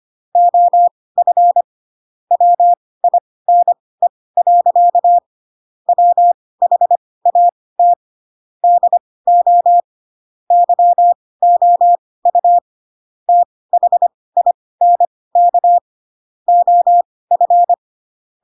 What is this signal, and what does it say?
Morse code, heard as a keyed tone, about 25 wpm, spaced as for 18 wpm, 700 Hz. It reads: OF WINE. WHAT DO YOU THINK OF
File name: Pinocchio / Pinocchio ch2 12 25 ik1hge